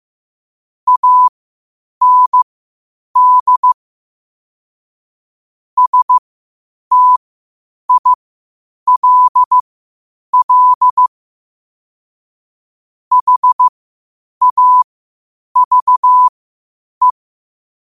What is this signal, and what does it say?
Morse code, heard as a keyed tone, about 15 wpm, spaced as for 8 wpm, 1000 Hz. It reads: AND STILL HAVE
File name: qotdfiles / 8wpm